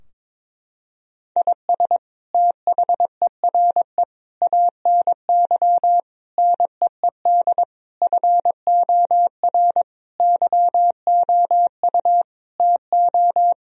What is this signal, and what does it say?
Morse code, heard as a keyed tone, about 22 wpm, 700 Hz. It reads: IS THERE ANY NEED FOR YOU TO
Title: Sentences from Top 1000 Words - Review Part 4 - 22wpm